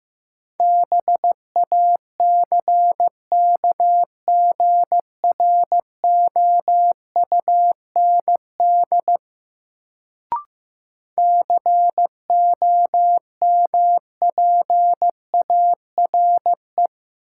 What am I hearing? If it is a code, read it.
Morse code, heard as a keyed tone, about 16 wpm, 700 Hz. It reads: BACKGROUND COMPARE
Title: Sentences from Top 1300 Words - New Word Drill - 15wpm